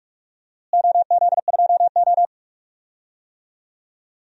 Morse code, heard as a keyed tone, about 41 wpm, 700 Hz. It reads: OZ2O
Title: DXCC-100 Call Signs - 4 Characters - Repeated 2x - 45wpm